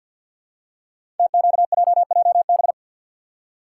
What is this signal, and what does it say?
Morse code, heard as a keyed tone, about 52 wpm, 700 Hz. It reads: TQJJB
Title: Code Groups - 5 Letter:Numbers - 50wpm